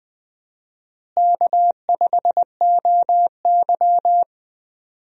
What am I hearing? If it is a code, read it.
Morse code, heard as a keyed tone, about 20 wpm, 700 Hz. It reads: K5OY
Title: USA Call Signs - 4 Characters - Repeated 2x - 20wpm